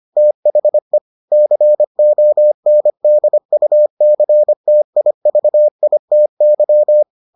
Morse code, heard as a keyed tone, about 25 wpm, 600 Hz. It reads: THE CONDUCTIVITY